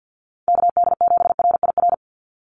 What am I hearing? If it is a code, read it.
Morse code, heard as a keyed tone, about 50 wpm, 700 Hz. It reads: KD7RIR
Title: USA Call Signs 50wpm